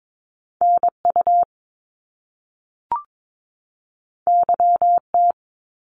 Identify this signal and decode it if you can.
Morse code, heard as a keyed tone, about 22 wpm, 700 Hz. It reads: NU YT
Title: Canadian Provinces and Territories 22wpm